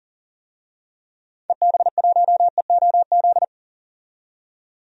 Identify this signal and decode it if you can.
Morse code, heard as a keyed tone, about 37 wpm, 700 Hz. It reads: EB1EOZ
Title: DXCC-100 Call Signs - 6 Characters - 40wpm